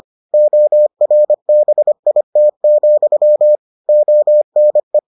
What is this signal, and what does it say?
Morse code, heard as a keyed tone, about 25 wpm, 600 Hz. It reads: ORBIT, ONE